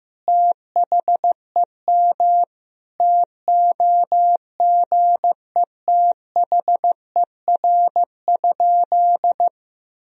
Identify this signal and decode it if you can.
Morse code, heard as a keyed tone, about 15 wpm, 700 Hz. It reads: THEM TOGETHER?
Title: Sentences from Top 1000 Words - Review Part 1 - 15wpm